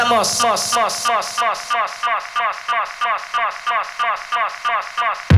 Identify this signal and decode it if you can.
Speech. There is no code.